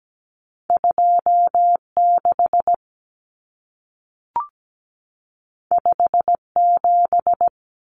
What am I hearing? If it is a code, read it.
Morse code, heard as a keyed tone, about 17 wpm, 700 Hz. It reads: 2 6 5 7
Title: Common Cut Numbers - 2-Digit 17wpm